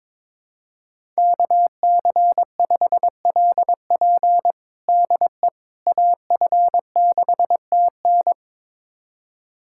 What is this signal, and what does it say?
Morse code, heard as a keyed tone, about 22 wpm, 700 Hz. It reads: KC5LP DE AF6TN
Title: USA Callsign DE Callsign - 5 Characters - 22wpm